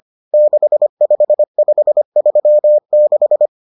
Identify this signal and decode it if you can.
Morse code, heard as a keyed tone, about 25 wpm, 600 Hz. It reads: 65536